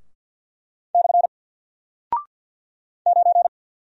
Morse code, heard as a keyed tone, about 48 wpm, 700 Hz. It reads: / 9
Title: Single Character (Letter:Number:Punctuation:Prosign) Alt 50wpm